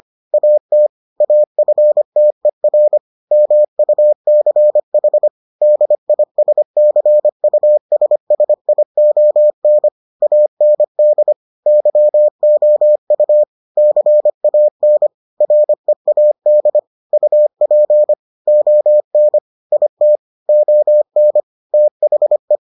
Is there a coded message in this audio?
Yes: Morse code from 0.3 to 22.6 s, heard as a keyed tone, about 25 wpm, 600 Hz. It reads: AT AFTER MUCH DISCUSSION AND YOU CAN READ UP ON IT ON THE